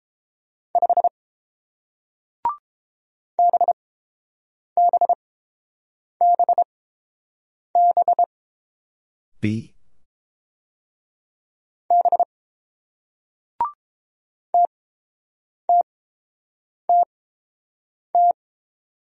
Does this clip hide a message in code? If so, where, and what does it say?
Yes: Morse code from 0.8 to 18.3 s, heard as a keyed tone, about 30 wpm, spaced as for 7 wpm, 690 Hz. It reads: 5 BBBB B TTTT